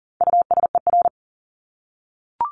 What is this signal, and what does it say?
Morse code, heard as a keyed tone, about 40 wpm, 700 Hz. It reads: USER
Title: Top 751-1000 Words 40wpm